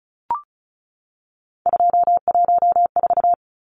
Morse code, heard as a keyed tone, about 34 wpm, 700 Hz. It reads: 214
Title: Numbers - 3-Digit 35wpm